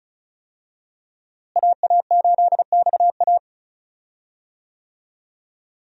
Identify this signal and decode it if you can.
Morse code, heard as a keyed tone, about 36 wpm, 700 Hz. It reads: AA8XA
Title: USA Call Signs - 5 Characters - Repeated 3x - 35wpm